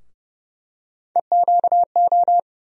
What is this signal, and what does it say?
Morse code, heard as a keyed tone, about 30 wpm, 700 Hz. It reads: EQO
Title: USA Call Sign Postfixes - 3 Letters 30wpm